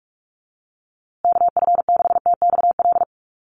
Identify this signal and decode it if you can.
Morse code, heard as a keyed tone, about 44 wpm, 700 Hz. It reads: KF6TXL
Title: USA Call Signs Repeated 3x 45wpm